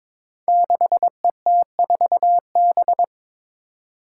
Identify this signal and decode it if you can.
Morse code, heard as a keyed tone, about 21 wpm, 700 Hz. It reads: 6ET4B
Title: Five Challenging Letter-Number Combinations 22wpm